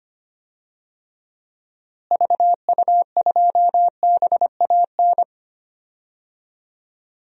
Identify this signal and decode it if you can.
Morse code, heard as a keyed tone, about 25 wpm, 700 Hz. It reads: VU2BAN
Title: DXCC-100 Call Signs - 6 Characters - Repeated 3x - 25wpm